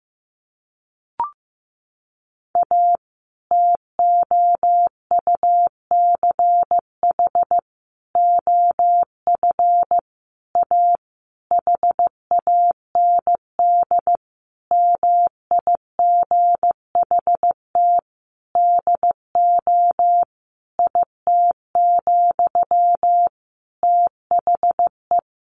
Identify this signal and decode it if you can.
Morse code, heard as a keyed tone, about 16 wpm, 700 Hz. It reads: A TOUCH OF A HAND MIGHT DO IT, THE
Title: A Sound Of Thunder 15wpm